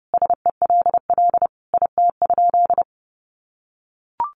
Morse code, heard as a keyed tone, about 30 wpm, 700 Hz. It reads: SELL IT?
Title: Sentences from Top 1100 Words 30wpm